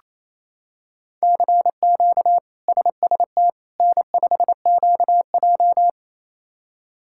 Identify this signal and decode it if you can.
Morse code, heard as a keyed tone, about 27 wpm, 700 Hz. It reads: CQ SST N5QJ
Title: K1USN SST QSO Practice - 28wpm